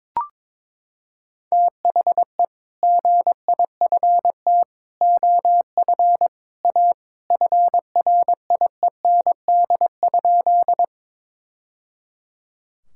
Morse code, heard as a keyed tone, about 22 wpm, 700 Hz. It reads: THE GIFT OF A FRIEND?
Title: Sentences from Top 1000 Words - Review Part 6 - 22wpm